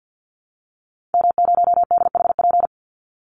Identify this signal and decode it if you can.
Morse code, heard as a keyed tone, about 50 wpm, 700 Hz. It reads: M0DHP
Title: Call Signs - 50wpm